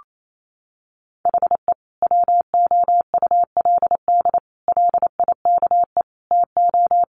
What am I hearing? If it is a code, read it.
Morse code, heard as a keyed tone, about 28 wpm, 700 Hz. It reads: HE WOULD LIKE TO
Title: Sentences from Top 1300 Words 28wpm